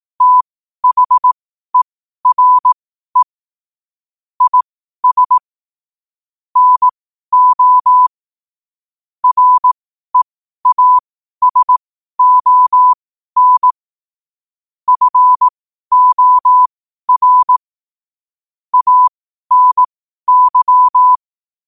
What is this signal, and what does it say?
Morse code, heard as a keyed tone, about 18 wpm, spaced as for 12 wpm, 1000 Hz. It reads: THERE IS NO REASON FOR ANY
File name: qotdfiles / 12wpm